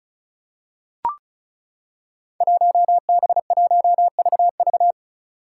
Morse code, heard as a keyed tone, about 36 wpm, 700 Hz. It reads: 1B1VV
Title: Five Challenging Letter-Number Combinations 35wpm